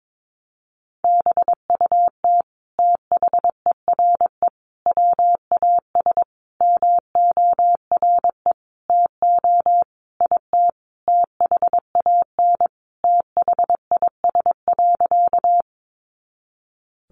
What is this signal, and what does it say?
Morse code, heard as a keyed tone, about 22 wpm, 700 Hz. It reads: BUT THERE WAS MORE TO IT THAN THIS.
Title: Sentences from Top 100 Words 22wpm